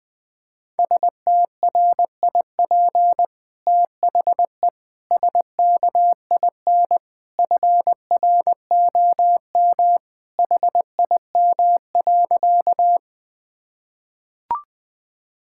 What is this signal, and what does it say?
Morse code, heard as a keyed tone, about 21 wpm, 700 Hz. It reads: STRIP THE SKIN FROM HIM.
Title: Sentences from Top 1600 Words 20wpm